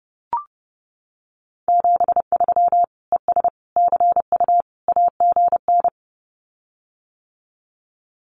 Morse code, heard as a keyed tone, about 28 wpm, 700 Hz. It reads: 73 ES CU AGN
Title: Common QSO Phrases Encore 30wpm